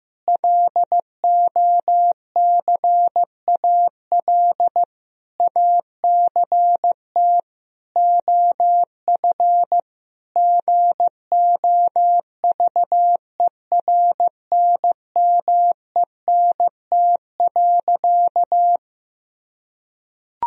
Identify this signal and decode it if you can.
Morse code, heard as a keyed tone, about 15 wpm, 700 Hz. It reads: LOCAL ACT OF GOVERNMENT.
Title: Sentences from Top 1000 Words - Review Part 3 - 15wpm